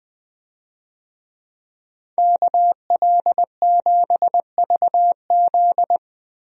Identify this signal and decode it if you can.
Morse code, heard as a keyed tone, about 20 wpm, 700 Hz. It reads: KL7VZ
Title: DXCC-100 Call Signs - 5 Characters - 20wpm